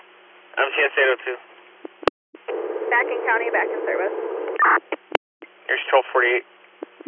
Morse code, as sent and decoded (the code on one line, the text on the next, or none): none